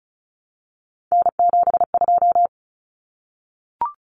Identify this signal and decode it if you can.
Morse code, heard as a keyed tone, about 34 wpm, 700 Hz. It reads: N72
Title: Common Cut Numbers - 3-Digit 35wpm